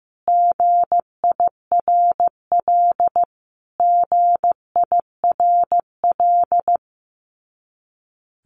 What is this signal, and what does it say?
Morse code, heard as a keyed tone, about 15 wpm, 700 Hz. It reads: GIRL GIRL
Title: Top 500 Words Repeated 15wpm